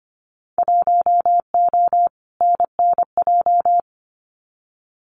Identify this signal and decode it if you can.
Morse code, heard as a keyed tone, about 25 wpm, 700 Hz. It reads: 1O NNJ
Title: Winter Field Day Exchanges 25wpm